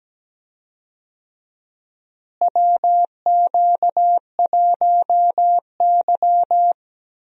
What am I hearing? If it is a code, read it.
Morse code, heard as a keyed tone, about 17 wpm, 700 Hz. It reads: WQ1Y
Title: USA Call Signs - 4 Characters - Repeated 2x - 17wpm